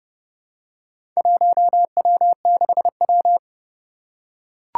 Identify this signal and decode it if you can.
Morse code, heard as a keyed tone, about 28 wpm, 690 Hz. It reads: 1W6W
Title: Four Challenging Letter-Number Combinations 30wpm